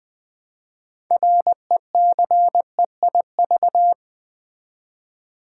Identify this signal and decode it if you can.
Morse code, heard as a keyed tone, about 20 wpm, 700 Hz. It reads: RECEIV
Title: Build A Word - 3 Letters At A Time - 8-Letter Words - 20wpm